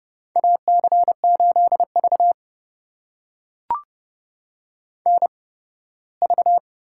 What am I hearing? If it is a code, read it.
Morse code, heard as a keyed tone, about 31 wpm, spaced as for 18 wpm, 700 Hz. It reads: AC8V NV
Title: USA Call Signs - 4 Characters - 8x Character Spacing - 30wpm